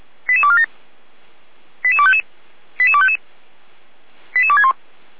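Five-tone selective call (ZVEI), heard as quick runs of five tones, tones of about 70 ms. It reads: 88247 88255 88255 88261
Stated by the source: ZVEI five-tone selective call received on an SDR.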